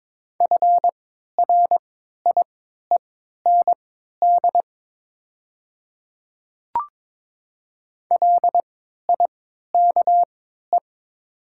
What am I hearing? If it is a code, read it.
Morse code, heard as a keyed tone, about 22 wpm, spaced as for 13 wpm, 690 Hz. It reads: FRIEND LIKE
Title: 3x Character Spacing - Top 300 Words - 22wpm